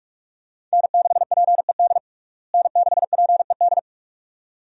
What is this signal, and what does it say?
Morse code, heard as a keyed tone, about 46 wpm, 700 Hz. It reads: N6PED N6PED
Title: USA Call Signs - 5 Characters - Repeated 2x - 45wpm